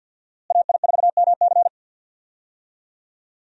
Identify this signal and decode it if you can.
Morse code, heard as a keyed tone, about 51 wpm, 700 Hz. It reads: AI4MC
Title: USA Call Signs - 5 Characters - Repeated 3x - 50wpm